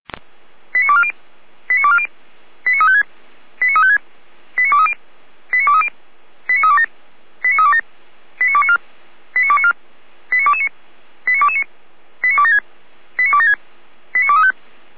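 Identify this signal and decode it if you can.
Five-tone selective call (ZVEI), heard as quick runs of five tones, tones of about 70 ms. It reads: 89240 89240 89356 89356 89230 89230 89327 89327 89294 89294 89229 89229 89276 89276 89235
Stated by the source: ZVEI five-tone selective call received on an SDR, recorded by feuerwehr-orlamuende (fire brigade Orlamünde).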